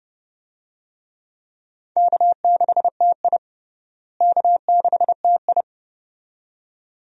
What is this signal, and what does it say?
Morse code, heard as a keyed tone, about 30 wpm, 700 Hz. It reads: K6TI K6TI
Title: USA Call Signs - 4 Characters - Repeated 2x - 30wpm